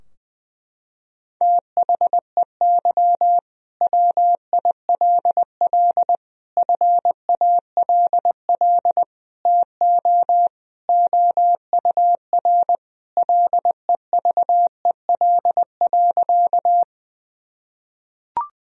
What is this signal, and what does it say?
Morse code, heard as a keyed tone, about 20 wpm, 700 Hz. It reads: THEY WILL FALL TO OUR LEVEL.
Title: Sentences from Top 1000 Words - Review Part 4 - 20wpm